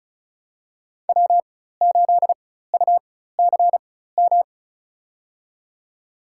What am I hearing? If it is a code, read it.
Morse code, heard as a keyed tone, about 34 wpm, spaced as for 16 wpm, 700 Hz. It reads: W8UCM
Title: USA Call Signs - 5 Characters - 4x Character Spacing - 35wpm